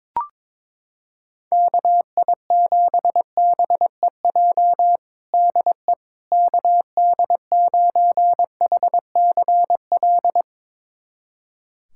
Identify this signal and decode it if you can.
Morse code, heard as a keyed tone, about 22 wpm, 700 Hz. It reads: KI7BEJ DE KD9HCL
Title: USA Callsign DE Callsign - 6 Characters - 22wpm